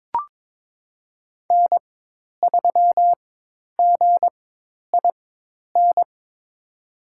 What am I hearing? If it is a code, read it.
Morse code, heard as a keyed tone, about 22 wpm, spaced as for 10 wpm, 700 Hz. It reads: N3GIN
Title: USA Call Signs - 5 Characters - 4x Character Spacing - 22wpm